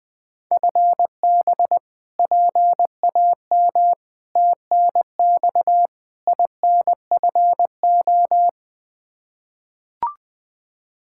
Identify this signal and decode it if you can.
Morse code, heard as a keyed tone, about 20 wpm, 700 Hz. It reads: FB PAM TNX INFO
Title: K1USN SST QSO Practice 20wpm